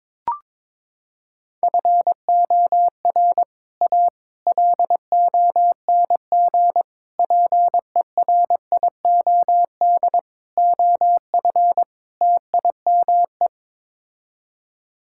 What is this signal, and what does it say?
Morse code, heard as a keyed tone, about 22 wpm, 700 Hz. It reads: FOR A LONG PERIOD OF TIME
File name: practice / 22wpm / Sets of 6 Words - 22wpm